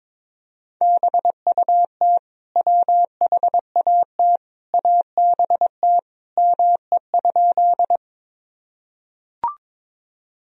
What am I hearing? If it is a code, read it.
Morse code, heard as a keyed tone, about 22 wpm, 700 Hz. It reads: BUT WHAT ABT ME?
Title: Sentences from Top 200 Words and Abbreviations 22wpm